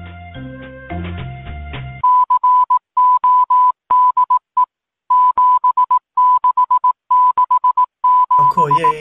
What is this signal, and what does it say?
Morse code, heard as a keyed tone, about 18 wpm, 1000 Hz, with speech. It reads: CODE 7667